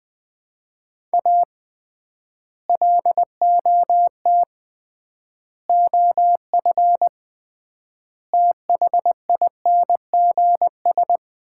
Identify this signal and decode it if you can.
Morse code, heard as a keyed tone, about 21 wpm, 700 Hz. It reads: A LOT OF THINGS
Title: 3x Word-Spacing - Sets of 4 Words - 20wpm